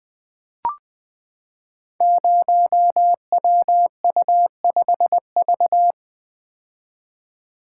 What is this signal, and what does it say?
Morse code, heard as a keyed tone, about 20 wpm, 700 Hz. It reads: 0WU5V
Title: Code Groups - 5 Letter:Numbers - 20wpm